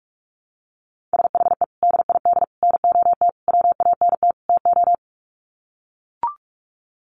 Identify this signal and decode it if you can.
Morse code, heard as a keyed tone, about 44 wpm, 700 Hz. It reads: SHE DID NOT WANT TO
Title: Sets of 5 Words 45wpm